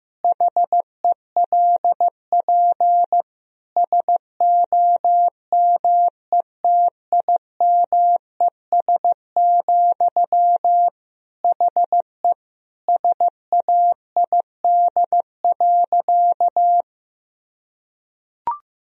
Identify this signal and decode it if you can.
Morse code, heard as a keyed tone, about 15 wpm, 700 Hz. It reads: HELP SOMETIMES, HE SAID.
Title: Sentences from Top 1000 Words - Review Part 3 - 15wpm